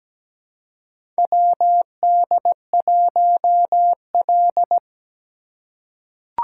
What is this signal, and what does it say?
Morse code, heard as a keyed tone, about 17 wpm, 700 Hz. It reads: WD1L
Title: USA Call Signs - 4 Characters - 4x Character Spacing - 17wpm